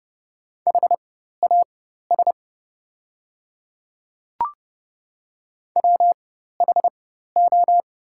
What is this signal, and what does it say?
Morse code, heard as a keyed tone, about 31 wpm, spaced as for 14 wpm, 700 Hz. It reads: HAS WHO
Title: 4x Character Spacing - Top 200 Words - 30wpm